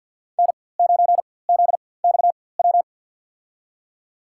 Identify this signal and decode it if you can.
Morse code, heard as a keyed tone, about 53 wpm, spaced as for 24 wpm, 700 Hz. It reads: N9ZXW
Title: USA Call Signs - 5 Characters - 4x Character Spacing - 50wpm